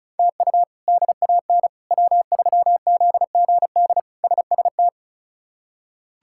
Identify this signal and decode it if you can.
Morse code, heard as a keyed tone, about 34 wpm, 700 Hz. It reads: TU DAN W3ZGD SST
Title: K1USN SST QSO Practice - 35wpm